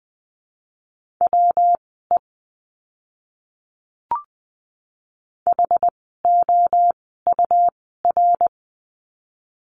Morse code, heard as a keyed tone, about 20 wpm, spaced as for 15 wpm, 700 Hz. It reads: WE HOUR
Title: 115 Lesson 19f - Letter w - Words 20wpm